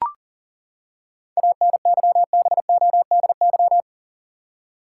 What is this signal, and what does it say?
Morse code, heard as a keyed tone, about 39 wpm, 700 Hz. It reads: ANYBODY
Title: Seven Letter Words - 40wpm